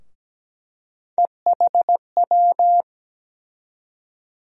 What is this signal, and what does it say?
Morse code, heard as a keyed tone, about 17 wpm, 700 Hz. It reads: EHW
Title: USA Call Sign Postfixes - 3 Letters 17wpm